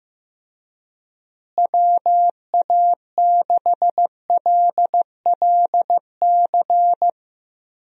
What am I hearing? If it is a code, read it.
Morse code, heard as a keyed tone, about 15 wpm, 700 Hz. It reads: WA6LLC
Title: USA Call Signs - 6 Characters - Repeated 3x - 15wpm